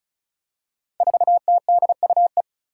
Morse code, heard as a keyed tone, about 36 wpm, 700 Hz. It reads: 4TDUE
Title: Five Challenging Letter-Number Combinations 35wpm